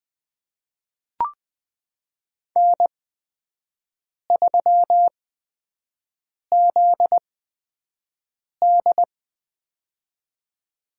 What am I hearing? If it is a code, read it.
Morse code, heard as a keyed tone, about 20 wpm, spaced as for 5 wpm, 700 Hz. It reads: N3ZD
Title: USA Call Signs - 4 Characters - 8x Character Spacing - 20wpm